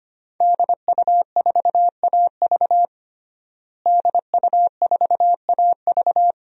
Morse code, heard as a keyed tone, about 25 wpm, 700 Hz. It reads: DU4AV DU4AV
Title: DXCC-100 Call Signs - 5 Characters - Repeated 2x - 25wpm